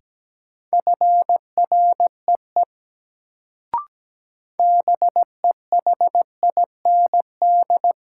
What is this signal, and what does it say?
Morse code, heard as a keyed tone, about 17 wpm, 700 Hz. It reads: FREE BEHIND
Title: Sentences from Top 500 Words - New Word Drill - 17wpm